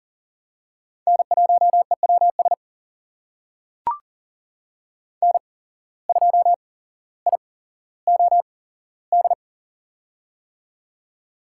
Morse code, heard as a keyed tone, about 38 wpm, spaced as for 17 wpm, 700 Hz. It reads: N1EWS N2IOD
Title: USA Call Signs - 5 Characters - 8x Character Spacing - 40wpm